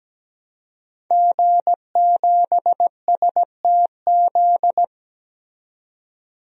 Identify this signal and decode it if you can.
Morse code, heard as a keyed tone, about 17 wpm, 700 Hz. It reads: G7STZ
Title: DXCC-100 Call Signs - 5 Characters - 17wpm